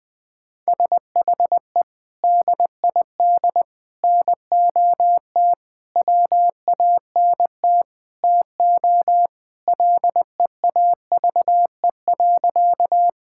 Morse code, heard as a keyed tone, about 19 wpm, 700 Hz. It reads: SHE DID NOT WANT TO LEAVE.
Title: Sentences with 5-Word Phrases - 20wpm